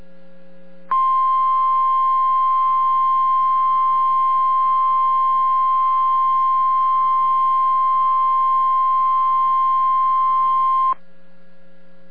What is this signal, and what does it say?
Emergency Alert System (SAME) tones (the 1050 Hz attention tone). It is heard from 0.9 to 11.0 s.